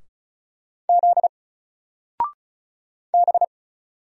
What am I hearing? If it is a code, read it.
Morse code, heard as a keyed tone, about 36 wpm, 700 Hz. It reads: Z B
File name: Single-Letter-Number-35wpm